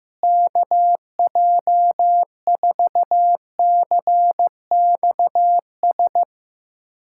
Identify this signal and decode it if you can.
Morse code, heard as a keyed tone, about 15 wpm, 700 Hz. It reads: KJ4CXS
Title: USA Callsign DE Callsign - 6 Characters - 15wpm